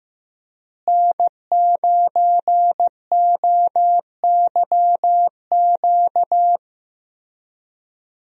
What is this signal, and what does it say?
Morse code, heard as a keyed tone, about 15 wpm, 700 Hz. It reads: N9OYQ
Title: USA Call Signs - 5 Characters - Repeated 3x - 15wpm